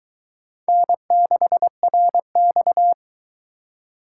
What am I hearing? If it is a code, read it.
Morse code, heard as a keyed tone, about 23 wpm, 700 Hz. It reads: N6RX